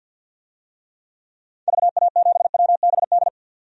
Morse code, heard as a keyed tone, about 53 wpm, 700 Hz. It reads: VA7WBD
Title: DXCC-100 Call Signs - 6 Characters - 50wpm